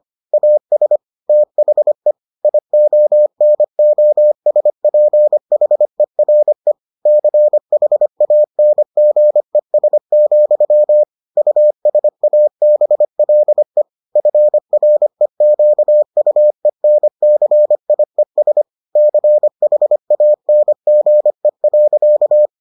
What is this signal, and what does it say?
Morse code, heard as a keyed tone, about 25 wpm, 600 Hz. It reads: AS THE IONOSPHERE CHANGES, USABLE FREQUENCIES CHANGE.